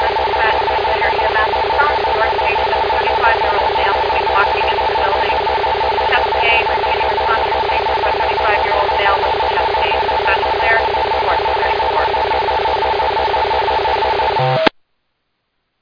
Two-tone teleprinter signal, about 450 Hz shift.
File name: Speculator EMS 2024 03 27 14 33 58